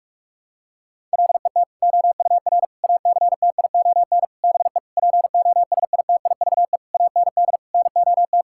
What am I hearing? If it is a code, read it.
Morse code, heard as a keyed tone, about 46 wpm, 700 Hz. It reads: LET OUR ACTION BE POSITIVE AND NOT
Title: Sentences from Top 1000 Words - Review Part 6 - 45wpm